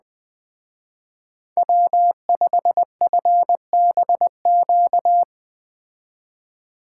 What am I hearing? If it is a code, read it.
Morse code, heard as a keyed tone, about 19 wpm, 700 Hz. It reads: W5FBQ
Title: Speed Racing - USA 1x3 Call Signs 17wpm